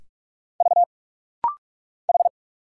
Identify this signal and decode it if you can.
Morse code, heard as a keyed tone, about 46 wpm, 690 Hz. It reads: V H
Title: Single Letters - Rapid-Fire 45wpm